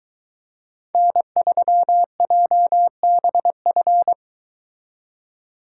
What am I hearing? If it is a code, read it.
Morse code, heard as a keyed tone, about 23 wpm, 700 Hz. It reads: N3JBF